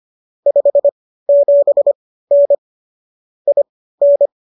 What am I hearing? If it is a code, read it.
Morse code, heard as a keyed tone, about 25 wpm, spaced as for 15 wpm, 570 Hz. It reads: 57N IN